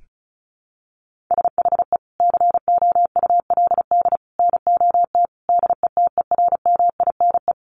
Morse code, heard as a keyed tone, about 34 wpm, 700 Hz. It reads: SHE COULD NOT DETERMINE
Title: Sentences from Top 700 Words 35wpm